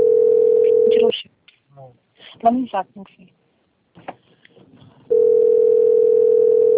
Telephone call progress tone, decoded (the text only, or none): ringback tone